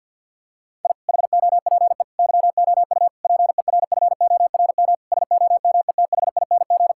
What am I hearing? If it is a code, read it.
Morse code, heard as a keyed tone, about 50 wpm, 700 Hz. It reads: I HOPE YOU PERFORM SOMETHING
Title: Sentences from Top 1000 Words - Review Part 6 - 50wpm